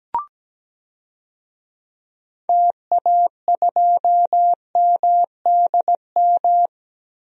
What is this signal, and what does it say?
Morse code, heard as a keyed tone, about 17 wpm, 700 Hz. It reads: TA2MDM
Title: DXCC-100 Call Signs - 6 Characters - Repeated 2x - 17wpm